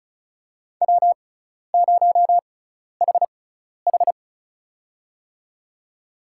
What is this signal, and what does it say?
Morse code, heard as a keyed tone, about 34 wpm, spaced as for 12 wpm, 700 Hz. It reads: W0HH
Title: USA Call Signs - 4 Characters - 6x Character Spacing - 35wpm